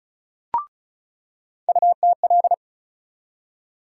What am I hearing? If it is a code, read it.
Morse code, heard as a keyed tone, about 36 wpm, 700 Hz. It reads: UTL
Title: USA Call Sign Postfixes - 3 Letters 35wpm